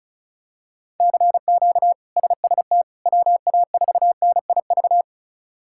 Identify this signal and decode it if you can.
Morse code, heard as a keyed tone, about 36 wpm, 700 Hz. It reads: CQ SST WA4NIV
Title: K1USN SST QSO Practice - 35wpm